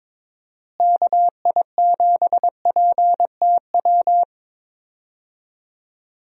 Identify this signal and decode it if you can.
Morse code, heard as a keyed tone, about 22 wpm, 700 Hz. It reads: KI7PTW
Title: USA Call Signs - 6 Characters - Repeated 2x - 22wpm